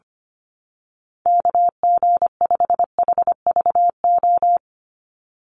Morse code, heard as a keyed tone, about 25 wpm, 700 Hz. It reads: KG5HVO
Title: Call Signs - 25wpm